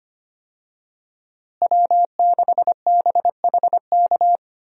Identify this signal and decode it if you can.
Morse code, heard as a keyed tone, about 25 wpm, 700 Hz. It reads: W6BHK
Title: USA Call Signs - 5 Characters - Repeated 2x - 25wpm